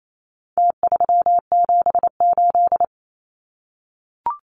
Morse code, heard as a keyed tone, about 28 wpm, 700 Hz. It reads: T378
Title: Numbers - Contest Style - 4-Digit 28wpm